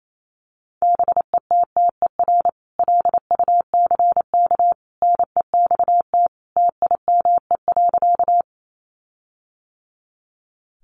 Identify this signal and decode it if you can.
Morse code, heard as a keyed tone, about 28 wpm, 700 Hz. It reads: BETTER LUCK NEXT TIME.
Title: Sentences from Top 1500 Words 28wpm